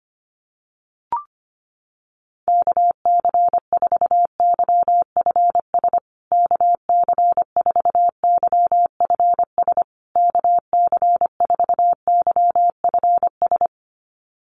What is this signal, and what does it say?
Morse code, heard as a keyed tone, about 25 wpm, 700 Hz. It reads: KC4YFS KC4YFS KC4YFS
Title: USA Call Signs Repeated 3x 25wpm